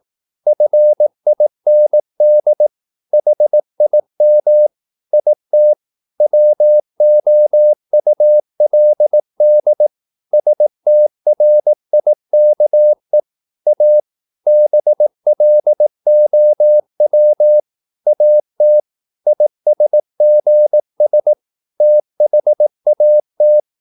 Morse code, heard as a keyed tone, about 18 wpm, 600 Hz. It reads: FIND HIM IT WOULD STRIKE A BLOW AT ISGS THAT
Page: Romanen Never af Ken Follet på engelsk.